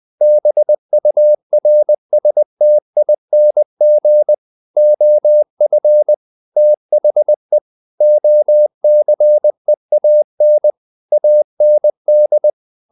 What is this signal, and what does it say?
Morse code, heard as a keyed tone, about 20 wpm, 600 Hz. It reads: BURSTING OF THE OCEAN AND
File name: the old man and the sea part 2 20wpm